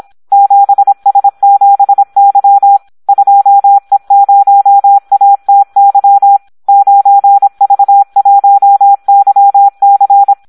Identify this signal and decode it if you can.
Morse code, heard as a keyed tone, about 26 wpm, 800 Hz. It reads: 7S7Y 2E0ATY 9V1YC
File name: Audio practica CW / SD 1 26